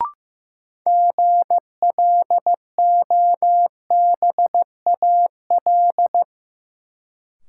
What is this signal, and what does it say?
Morse code, heard as a keyed tone, about 16 wpm, 700 Hz. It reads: GLOBAL
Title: Sentences from Top 1200 Words - New Word Drill Alt - 15wpm